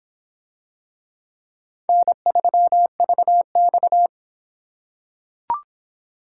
Morse code, heard as a keyed tone, about 24 wpm, 700 Hz. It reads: N3VX